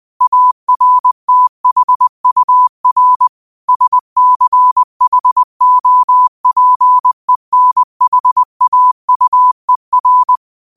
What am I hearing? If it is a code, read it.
Morse code, heard as a keyed tone, about 20 wpm, 1000 Hz. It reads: ARTHUR SCHOPENHAUER
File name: qotdfiles / 20wpm